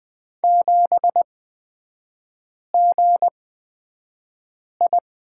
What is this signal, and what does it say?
Morse code, heard as a keyed tone, about 20 wpm, spaced as for 5 wpm, 700 Hz. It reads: 7GI
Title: 224 Lesson 36e - BT - Three Characters 20wpm